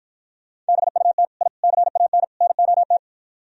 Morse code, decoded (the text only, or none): BUT I CAN NOT